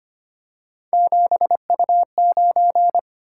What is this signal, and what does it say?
Morse code, heard as a keyed tone, about 25 wpm, 700 Hz. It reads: 7U9
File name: Three Letter-Number Combinations - 25wpm